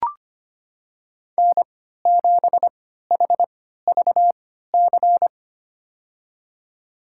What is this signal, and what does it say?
Morse code, heard as a keyed tone, about 25 wpm, spaced as for 14 wpm, 700 Hz. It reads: N7HVC
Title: USA Call Signs - 3x Character Spacing - 25wpm